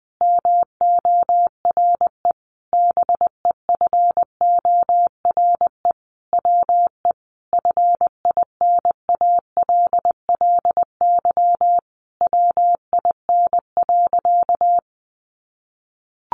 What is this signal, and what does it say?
Morse code, heard as a keyed tone, about 20 wpm, 700 Hz. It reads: MORE BEFORE WE FINALLY WIN.
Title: Sentences from Top 1500 Words 20wpm